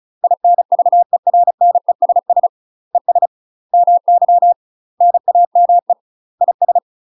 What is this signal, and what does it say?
Morse code, heard as a keyed tone, about 36 wpm, 700 Hz. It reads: INVERNESS ES MY NAME IS
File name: UK MASTER QSOs TRAINING / 35WPM / UK MASTER QSOs 0035